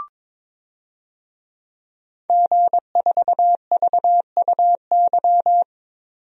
Morse code, heard as a keyed tone, about 22 wpm, 700 Hz. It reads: G4VUY
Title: DXCC-100 Call Signs - 5 Characters - 22wpm